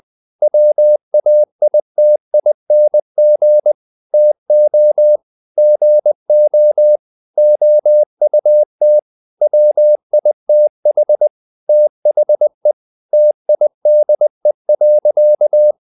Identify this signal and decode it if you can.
Morse code, heard as a keyed tone, about 20 wpm, 600 Hz. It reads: WAITING TO GO OUT WITH THE TIDE.
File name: the old man and the sea part 2 20wpm